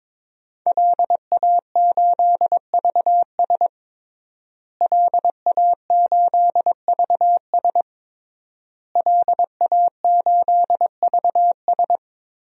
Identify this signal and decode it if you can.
Morse code, heard as a keyed tone, about 21 wpm, 700 Hz. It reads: LA8VS LA8VS LA8VS
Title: DXCC-100 Call Signs - 5 Characters - Repeated 3x - 22wpm